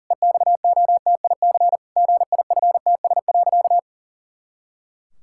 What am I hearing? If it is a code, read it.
Morse code, heard as a keyed tone, about 40 wpm, 700 Hz. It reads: EXOTIC GIFTS.
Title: Pangram Challenge v2 - 40wpm